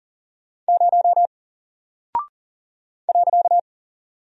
Morse code, heard as a keyed tone, about 40 wpm, 700 Hz. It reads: 0 .
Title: Single Character (Letter:Number:Punctuation:Prosign) Alt 40wpm